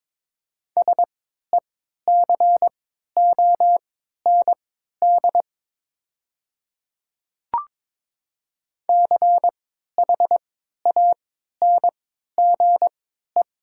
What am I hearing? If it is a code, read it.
Morse code, heard as a keyed tone, about 22 wpm, spaced as for 12 wpm, 700 Hz. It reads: SECOND CHANGE
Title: 3x Character Spacing - Top 500 Words - 22wpm